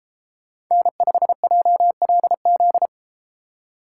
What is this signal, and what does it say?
Morse code, heard as a keyed tone, about 33 wpm, 700 Hz. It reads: N5JLZ